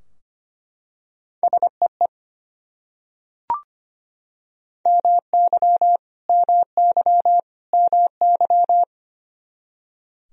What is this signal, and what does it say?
Morse code, heard as a keyed tone, about 25 wpm, 700 Hz. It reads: SEE MY MY MY
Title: Top 100 Words Repeated 25wpm